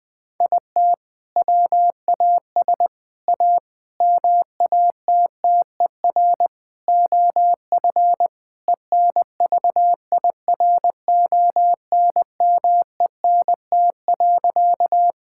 Morse code, heard as a keyed tone, about 20 wpm, 700 Hz. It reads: IT WAS A MATTER OF ENVIRONMENT.
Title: Sentences from Top 1000 Words - Review Part 5 - 20wpm